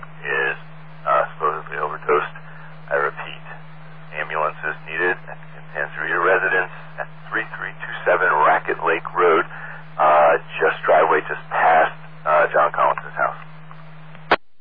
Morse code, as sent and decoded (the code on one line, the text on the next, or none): none